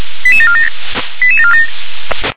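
Five-tone selective call (ZVEI), heard as quick runs of five tones, tones of about 80 ms. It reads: 88747
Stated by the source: ZVEI five-tone selective call received by SDR.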